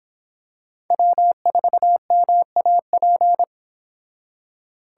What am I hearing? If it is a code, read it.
Morse code, heard as a keyed tone, about 26 wpm, 700 Hz. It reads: W4MAP